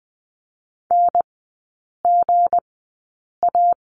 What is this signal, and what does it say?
Morse code, heard as a keyed tone, about 20 wpm, spaced as for 8 wpm, 690 Hz. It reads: NGA
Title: 159 Lesson 26e - Letter g - Three Characters 20wpm